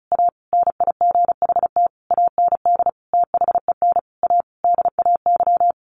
Morse code, heard as a keyed tone, about 34 wpm, 700 Hz. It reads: A NIGHT AND THEN A DAY
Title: A Sound Of Thunder 35wpm